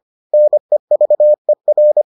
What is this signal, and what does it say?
Morse code, heard as a keyed tone, about 25 wpm, 600 Hz. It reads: NEVER